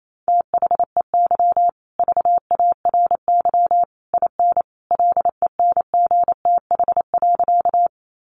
Morse code, heard as a keyed tone, about 28 wpm, 700 Hz. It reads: THEY VARY IN LENGTH.
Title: Sentences from Top 1200 Words 28wpm